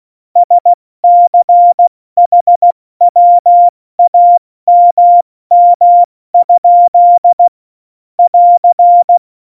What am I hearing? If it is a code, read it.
Morse code, heard as a keyed tone, about 16 wpm, spaced as for 14 wpm, 700 Hz. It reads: SCHWAMM? <AR>
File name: Morsekurs / Witze / Witz 22. Oktober 2019, 70 F80